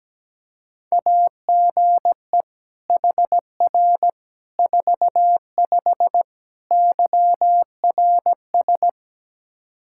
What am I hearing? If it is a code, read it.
Morse code, heard as a keyed tone, about 17 wpm, 700 Hz. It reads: AGE HR 45 YRS
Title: Practice QSO - Exchange 3 - More Info - 17wpm